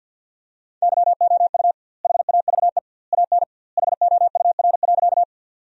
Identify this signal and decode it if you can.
Morse code, heard as a keyed tone, about 50 wpm, 700 Hz. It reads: YOU HAVE AN HOUR.
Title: 2x Word-Spacing - Sentences from Top 400 Words - 50wpm